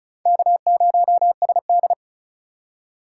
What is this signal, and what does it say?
Morse code, heard as a keyed tone, about 36 wpm, 700 Hz. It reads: K0SD
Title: USA Call Signs - 4 Characters - 8x Character Spacing - 35wpm